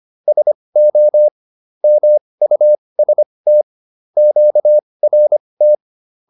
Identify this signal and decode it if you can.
Morse code, heard as a keyed tone, about 25 wpm, spaced as for 20 wpm, 600 Hz. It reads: SO MUST QRT